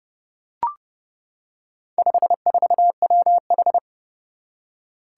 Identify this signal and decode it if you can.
Morse code, heard as a keyed tone, about 30 wpm, 700 Hz. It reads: 54WH
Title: Four Challenging Letter-Number Combinations 30wpm